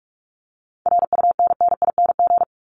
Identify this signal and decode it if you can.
Morse code, heard as a keyed tone, about 44 wpm, 700 Hz. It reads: RUNNING
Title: ING Suffix - Top 100 Words with Base Words 45wpm